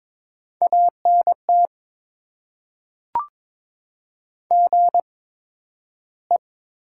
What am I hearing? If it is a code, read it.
Morse code, heard as a keyed tone, about 22 wpm, spaced as for 12 wpm, 700 Hz. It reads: ANT GE
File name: Top 20 QSO Elements - 8x Character Spacing - 22wpm